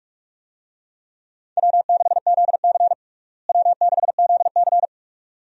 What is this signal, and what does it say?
Morse code, heard as a keyed tone, about 46 wpm, 700 Hz. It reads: W6ZC W6ZC
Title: USA Call Signs - 4 Characters - Repeated 2x - 45wpm